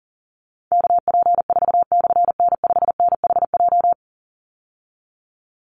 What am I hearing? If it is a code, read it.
Morse code, heard as a keyed tone, about 37 wpm, 700 Hz. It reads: KP4/N5NHJ
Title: Call Signs Repeated 3x - 40wpm